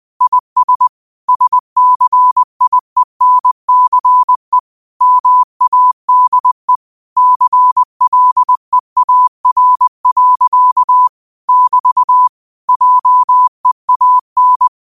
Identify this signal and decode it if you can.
Morse code, heard as a keyed tone, about 20 wpm, 1000 Hz. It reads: IS SCIENCE MADE CLEAR. <BT> JEAN
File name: qotdfiles / 20wpm